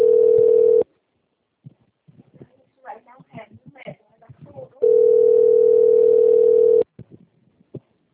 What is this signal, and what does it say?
Ringback tone (North America), 440 + 480 Hz, with speech.
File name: out- - - - - .2756